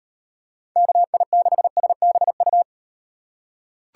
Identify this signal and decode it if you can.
Morse code, heard as a keyed tone, about 36 wpm, 700 Hz. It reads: KI6SBU